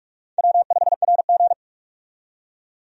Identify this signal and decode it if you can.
Morse code, heard as a keyed tone, about 46 wpm, 700 Hz. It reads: W5RG
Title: USA Call Signs - 4 Characters - 45wpm